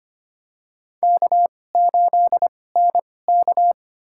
Morse code, heard as a keyed tone, about 25 wpm, spaced as for 18 wpm, 700 Hz. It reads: K8NK
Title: USA Call Signs - 4 Characters - 2x Character Spacing - 25wpm